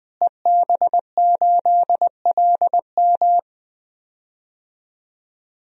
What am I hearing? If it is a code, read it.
Morse code, heard as a keyed tone, about 20 wpm, 700 Hz. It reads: EB8LM
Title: DXCC-100 Call Signs - 5 Characters - Repeated 3x - 20wpm